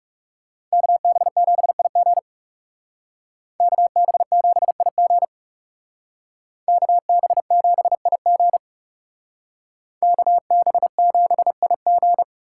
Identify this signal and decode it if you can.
Morse code, heard as a keyed tone, about 36 wpm, 700 Hz. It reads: KB7IG KB7IG KB7IG KB7IG
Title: Speed Racing - USA 2x2 Call Signs 30wpm